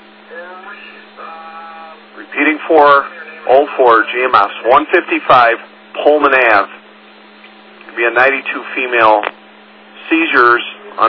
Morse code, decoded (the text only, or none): none